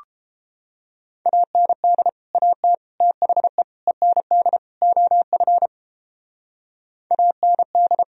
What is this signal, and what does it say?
Morse code, heard as a keyed tone, about 31 wpm, 700 Hz. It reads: AND AT THE END OF AND